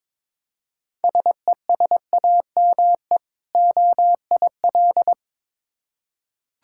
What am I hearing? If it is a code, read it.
Morse code, heard as a keyed tone, about 22 wpm, 690 Hz. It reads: SESAME OIL
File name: Difficult Words with Hints - 22wpm